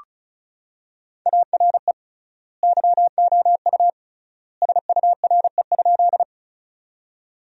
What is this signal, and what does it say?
Morse code, heard as a keyed tone, about 36 wpm, 700 Hz. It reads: ARE YOU SURE?
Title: 3x Word-Spacing - Sentences from Top 300 Words - 35wpm